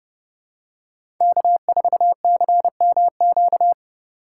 Morse code, heard as a keyed tone, about 32 wpm, 690 Hz. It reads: K4CMQ
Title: USA Call Signs - 5 Characters - 2x Character Spacing - 30wpm